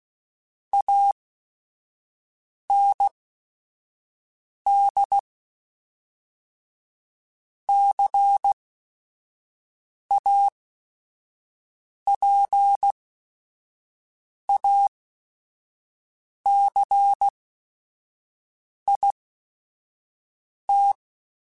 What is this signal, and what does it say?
Morse code, heard as a keyed tone, about 16 wpm, spaced as for 5 wpm, 800 Hz. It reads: AND CAPACIT
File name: Technical Subjects / Technical Subjects-mp3s / Reactance @5